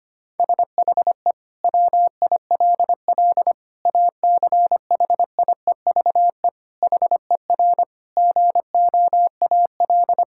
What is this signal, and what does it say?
Morse code, heard as a keyed tone, about 25 wpm, 700 Hz. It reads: SHE WILL ACHIEVE HER GOAL
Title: Sentences from Top 1000 Words - Review Part 6 - 25wpm